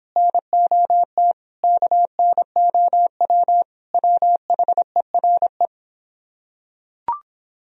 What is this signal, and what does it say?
Morse code, heard as a keyed tone, about 25 wpm, 700 Hz. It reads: NOT KNOW WHERE